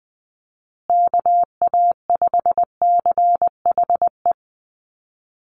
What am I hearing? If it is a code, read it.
Morse code, heard as a keyed tone, about 20 wpm, 700 Hz. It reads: KA5CHE
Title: USA Call Signs Repeated 3x 20wpm